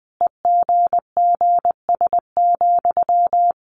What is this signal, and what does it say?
Morse code, heard as a keyed tone, about 20 wpm, 700 Hz. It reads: EGGS,
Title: A Sound Of Thunder 20wpm